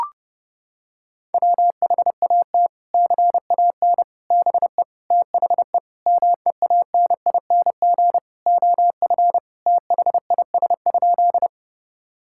Morse code, heard as a keyed tone, about 32 wpm, 700 Hz. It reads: WHAT CAN BE THE MEANING OF THIS?
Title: Sentences from Top 1000 Words - Review Part 6 - 30wpm